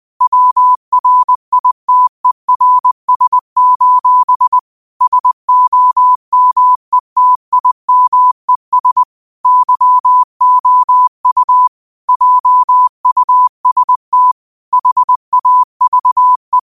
Morse code, heard as a keyed tone, about 20 wpm, 1000 Hz. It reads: WRITERS: SOMETIMES YOU JUST HAVE
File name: qotdfiles / 20wpm